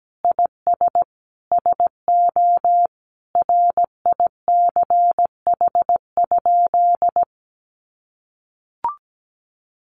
Morse code, heard as a keyed tone, about 17 wpm, 700 Hz. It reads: IS SO RICH?
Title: Sentences from Top 1100 Words 17wpm